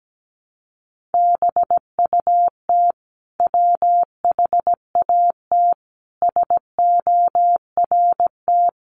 Morse code, heard as a keyed tone, about 17 wpm, 700 Hz. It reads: BUT WHAT SORT
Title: Sentences from Top 1300 Words 17wpm